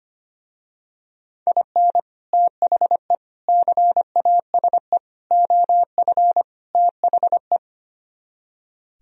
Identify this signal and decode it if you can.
Morse code, heard as a keyed tone, about 26 wpm, 700 Hz. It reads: IN THE CASE OF THE
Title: Speed-Racing - Sets of 5 Words 25wpm